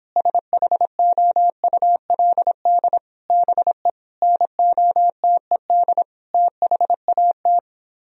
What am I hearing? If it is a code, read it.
Morse code, heard as a keyed tone, about 29 wpm, 700 Hz. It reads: SHOULD BE NOTED THAT